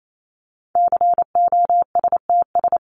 Morse code, heard as a keyed tone, about 28 wpm, 700 Hz. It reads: COSTS
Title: ES:S Suffix - Top 100 Words with Base Words 28wpm